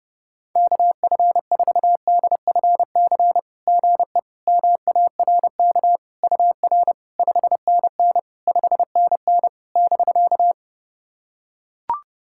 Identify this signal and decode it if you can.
Morse code, heard as a keyed tone, about 30 wpm, 700 Hz. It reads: KF4DFC GE MARK UR 5NN 5NN <BK>
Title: SOTA QSO Practice 30wpm